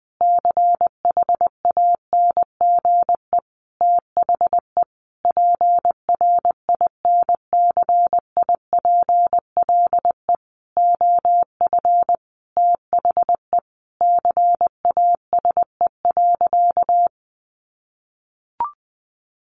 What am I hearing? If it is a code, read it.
Morse code, heard as a keyed tone, about 20 wpm, 700 Hz. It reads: CHANGE THE PRINCIPLE OF THE CASE.
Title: Sentences from Top 1400 Words 20wpm